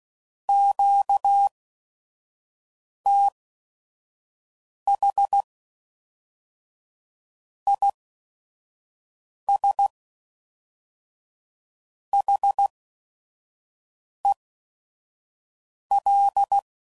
Morse code, heard as a keyed tone, about 16 wpm, spaced as for 5 wpm, 800 Hz. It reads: QTH IS HEL
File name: CW Code Groups @5WPM / QSOs / WD0FBT 1